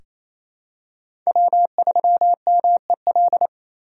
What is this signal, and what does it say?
Morse code, heard as a keyed tone, about 28 wpm, 700 Hz. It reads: W3MEL
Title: K1USN SST QSO Practice - 28wpm